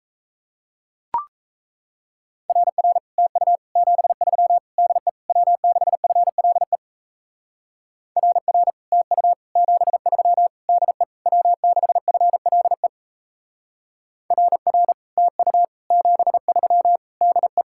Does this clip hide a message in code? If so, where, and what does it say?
Yes: Morse code from 2.5 to 17.6 s, heard as a keyed tone, about 36 wpm, 700 Hz. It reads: RR TU 73 DE W6FLE RR TU 73 DE W6FLE RR TU 73 DE